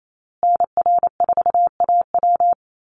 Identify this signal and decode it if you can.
Morse code, heard as a keyed tone, about 28 wpm, 700 Hz. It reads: NR4AW
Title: USA Call Signs Repeated 3x 28wpm